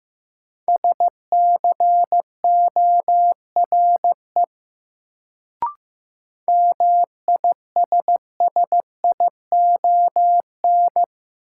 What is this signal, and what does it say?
Morse code, heard as a keyed tone, about 15 wpm, 700 Hz. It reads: SCORE MISSION
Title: Sentences from Top 1600 Words - New Word Drill - 15wpm